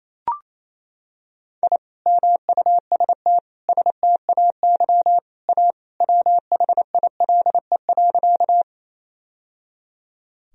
Morse code, heard as a keyed tone, about 28 wpm, 700 Hz. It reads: I MUST STAY A WHILE.
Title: Sentences from Top 1000 Words - Review Part 1 - 28wpm